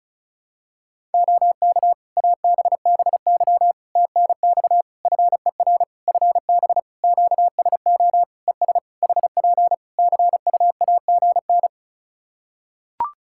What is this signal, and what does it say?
Morse code, heard as a keyed tone, about 34 wpm, 700 Hz. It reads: OK ABBY TNX FER FB QSO ES HP CUAGN
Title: Practice QSO - Exchange 4 - Ending - 35wpm